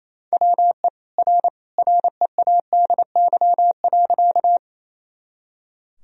Morse code, heard as a keyed tone, about 27 wpm, 700 Hz. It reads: WE R READY.
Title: Sentences from Top 400 Words and Abbreviations 28wpm